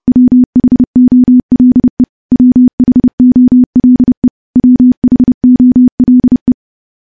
Morse code, heard as a keyed tone, about 30 wpm, 250 Hz. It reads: WHOLE WHOLE WHOLE
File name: morse / morse-training-30WPM